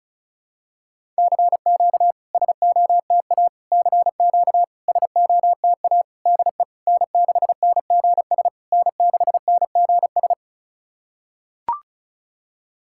Morse code, heard as a keyed tone, about 36 wpm, 700 Hz. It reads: CQ SOTA CQ SOTA DE N6NGS N6NGS
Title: SOTA QSO Practice 35wpm